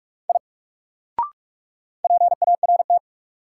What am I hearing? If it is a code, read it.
Morse code, heard as a keyed tone, about 45 wpm, 700 Hz. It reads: I PART
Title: Sentences from Top 200 Words - New Word Drill - 45wpm